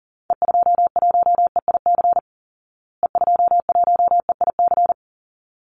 Morse code, heard as a keyed tone, about 40 wpm, 700 Hz. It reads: E21EIC E21EIC
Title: Call Signs Repeated 3x - 40wpm